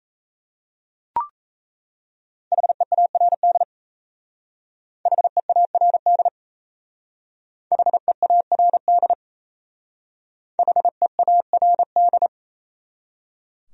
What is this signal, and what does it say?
Morse code, heard as a keyed tone, about 34 wpm, 700 Hz. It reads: HEARD HEARD HEARD HEARD
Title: Speed-Racing Top 500 Words 28wpm